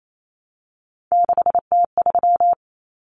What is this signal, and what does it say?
Morse code, heard as a keyed tone, about 28 wpm, 700 Hz. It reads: 6T3
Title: Common Cut Numbers - 3-Digit 28wpm